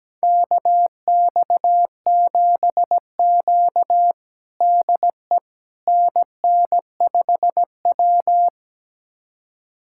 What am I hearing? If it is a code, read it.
Morse code, heard as a keyed tone, about 17 wpm, 700 Hz. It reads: KX7Q DE NN5W
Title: USA Callsign DE Callsign - 4 Characters - 17wpm